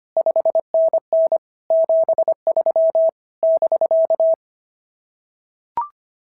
Morse code, heard as a keyed tone, about 25 wpm, 650 Hz. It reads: 5NN 73 <BK>
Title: SOTA QSO Practice 25wpm